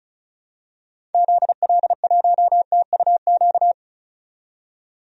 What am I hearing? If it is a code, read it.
Morse code, heard as a keyed tone, about 36 wpm, 700 Hz. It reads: ZL1TUQ
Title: DXCC-100 Call Signs - 6 Characters - 35wpm